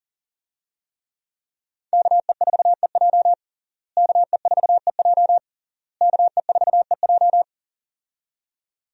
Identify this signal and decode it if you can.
Morse code, heard as a keyed tone, about 40 wpm, 700 Hz. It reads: KE4EJ KE4EJ KE4EJ
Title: USA Call Signs - 5 Characters - Repeated 3x - 40wpm